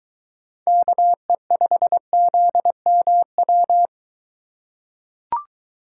Morse code, heard as a keyed tone, about 23 wpm, 700 Hz. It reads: KE5ZMW